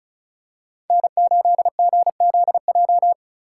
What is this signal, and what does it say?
Morse code, heard as a keyed tone, about 34 wpm, 700 Hz. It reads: N8GZJ
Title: USA Call Signs - 5 Characters - 2x Character Spacing - 35wpm